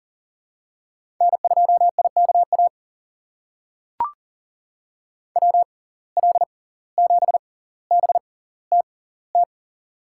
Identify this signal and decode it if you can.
Morse code, heard as a keyed tone, about 40 wpm, spaced as for 19 wpm, 700 Hz. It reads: N2IKA WL7BTT
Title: USA Call Signs - 6x Character Spacing - 40wpm